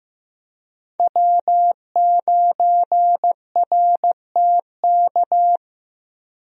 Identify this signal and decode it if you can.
Morse code, heard as a keyed tone, about 15 wpm, 700 Hz. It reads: W9RTK
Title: USA Call Signs - 5 Characters - Repeated 2x - 15wpm